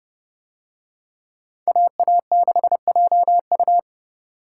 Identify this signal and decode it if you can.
Morse code, heard as a keyed tone, about 29 wpm, 700 Hz. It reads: AA6JU